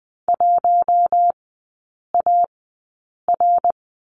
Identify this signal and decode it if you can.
Morse code, heard as a keyed tone, about 20 wpm, spaced as for 8 wpm, 700 Hz. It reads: 1AR
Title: 138 Lesson 23e - Letter f - Three Characters 20wpm